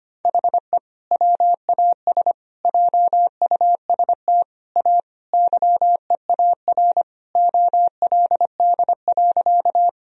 Morse code, heard as a keyed tone, about 25 wpm, 700 Hz. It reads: HE WAS JUST A YEAR OLD.
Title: Sentences from Top 1000 Words - Review Part 1 - 25wpm